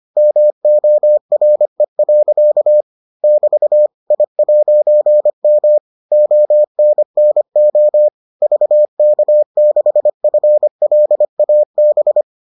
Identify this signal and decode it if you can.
Morse code, heard as a keyed tone, about 25 wpm, 600 Hz. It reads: MORE. <BT> I'M ONNO VK6FLAB